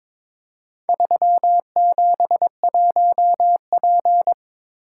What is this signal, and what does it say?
Morse code, heard as a keyed tone, about 22 wpm, 700 Hz. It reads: 371P
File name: practice / 22wpm / Four Letter-Number Combinations - 22wpm